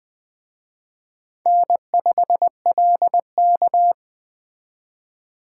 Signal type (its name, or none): Morse code